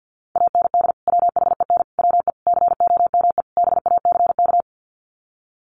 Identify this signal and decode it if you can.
Morse code, heard as a keyed tone, about 48 wpm, 700 Hz. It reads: AND WHEN WE COME BACK
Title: Sets of 5 Words 50wpm